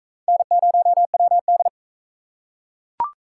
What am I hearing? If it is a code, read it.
Morse code, heard as a keyed tone, about 40 wpm, 700 Hz. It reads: N0WD